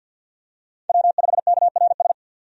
Morse code, heard as a keyed tone, about 45 wpm, 700 Hz. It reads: W5KRS
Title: USA Call Signs - 5 Characters - 50wpm